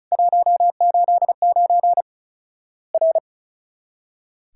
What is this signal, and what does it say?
Morse code, heard as a keyed tone, about 36 wpm, 700 Hz. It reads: 189 R
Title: Worked All Europe DX Contest - QTC Exchange - 35wpm